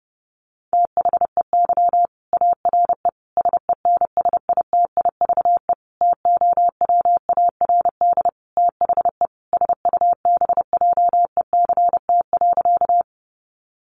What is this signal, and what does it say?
Morse code, heard as a keyed tone, about 30 wpm, 700 Hz. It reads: THEY ARE SENSITIVE TOWARD THE SUBJECT.
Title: Sentences from Top 1400 Words 30wpm